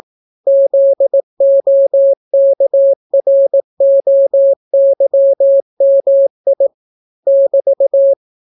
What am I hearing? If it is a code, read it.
Morse code, heard as a keyed tone, about 18 wpm, 550 Hz. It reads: ZOKROYMI <BT>